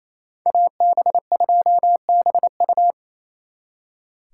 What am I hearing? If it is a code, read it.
Morse code, heard as a keyed tone, about 33 wpm, 700 Hz. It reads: AB2BU
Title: Speed Racing - USA Call Signs 28wpm